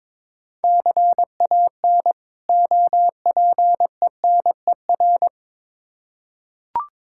Morse code, heard as a keyed tone, about 22 wpm, 700 Hz. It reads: CAN OPENER
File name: Difficult Words with Hints - 22wpm